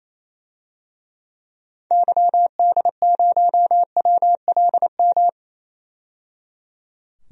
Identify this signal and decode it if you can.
Morse code, heard as a keyed tone, about 27 wpm, 700 Hz. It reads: YD0WLM
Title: DXCC-100 Call Signs - 6 Characters - 28wpm